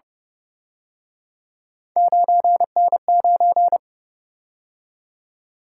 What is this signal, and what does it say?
Morse code, heard as a keyed tone, about 29 wpm, 700 Hz. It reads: 9N9
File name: Speed-Racing - Three Letter-Number Combinations - 25wpm